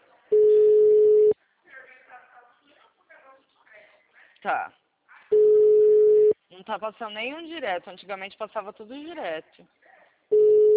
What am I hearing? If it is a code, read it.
Ringback tone (Europe, 425 Hz), with speech.